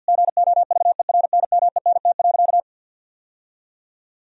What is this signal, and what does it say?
Morse code, heard as a keyed tone, about 48 wpm, 700 Hz. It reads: GOVERNMENT.
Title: Sentences from Top 1000 Words - Review Part 3 - 50wpm